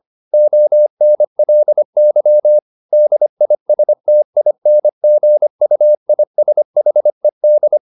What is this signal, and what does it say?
Morse code, heard as a keyed tone, about 25 wpm, 600 Hz. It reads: ONLY DISTINGUISHED
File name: the old man and the sea part 1 25wpm